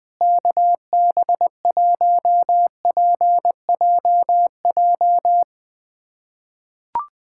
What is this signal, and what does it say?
Morse code, heard as a keyed tone, about 20 wpm, 700 Hz. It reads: KB1PJJ
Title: USA Callsign DE Callsign 20wpm